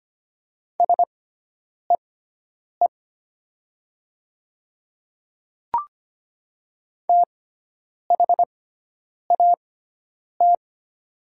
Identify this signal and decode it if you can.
Morse code, heard as a keyed tone, about 25 wpm, spaced as for 9 wpm, 700 Hz. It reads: SEE THAT
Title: 6x Character Spacing - Top 100 Words - 25wpm